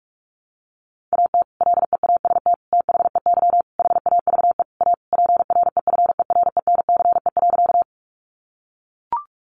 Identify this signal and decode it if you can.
Morse code, heard as a keyed tone, about 44 wpm, 700 Hz. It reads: AT LEAST THEY HAVE A PREFERENCE.
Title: Sentences from Top 1500 Words 45wpm